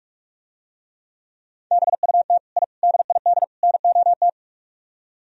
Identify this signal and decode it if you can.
Morse code, heard as a keyed tone, about 46 wpm, 700 Hz. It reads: BUT I DID NOT